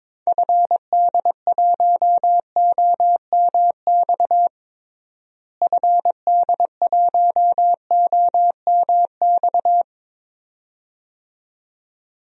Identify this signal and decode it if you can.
Morse code, heard as a keyed tone, about 21 wpm, 700 Hz. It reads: FD1OMX FD1OMX
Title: DXCC-100 Call Signs - 6 Characters - Repeated 2x - 22wpm